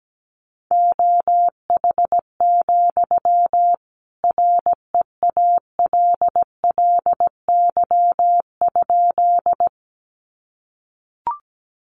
Morse code, heard as a keyed tone, about 17 wpm, 700 Hz. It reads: OH, REALLY?
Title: Sentences from Top 600 Words 17wpm